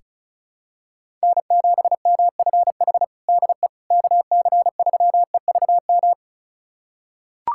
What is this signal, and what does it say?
Morse code, heard as a keyed tone, about 34 wpm, 700 Hz. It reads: N7MFH DE KC3EVM
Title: USA Callsign DE Callsign 35wpm